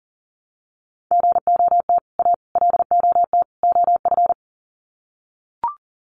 Morse code, heard as a keyed tone, about 40 wpm, 700 Hz. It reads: GOT A LOT OF
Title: Sets of 4 Words 40wpm